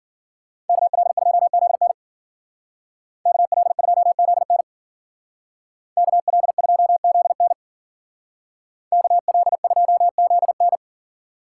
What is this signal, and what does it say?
Morse code, heard as a keyed tone, about 50 wpm, 700 Hz. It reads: KL2ZN KL2ZN KL2ZN KL2ZN